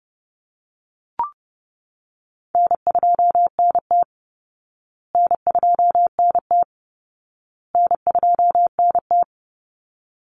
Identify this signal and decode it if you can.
Morse code, heard as a keyed tone, about 30 wpm, 700 Hz. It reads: N2NT N2NT N2NT
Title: Easy Call Signs Repeated 3x - 30wpm